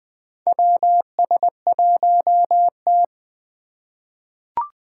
Morse code, heard as a keyed tone, about 21 wpm, 700 Hz. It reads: WS1T
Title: USA Call Signs - 4 Characters - 4x Character Spacing - 20wpm